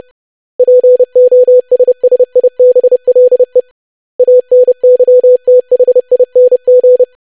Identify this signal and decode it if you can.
Morse code, heard as a keyed tone, about 30 wpm, 500 Hz. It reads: POSSIBLE ANYTHING
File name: lesson / web / basic500-25wpm